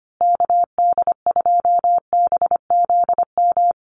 Morse code, heard as a keyed tone, about 25 wpm, 700 Hz. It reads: KD2BZM
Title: USA Call Signs Repeated 2x 25wpm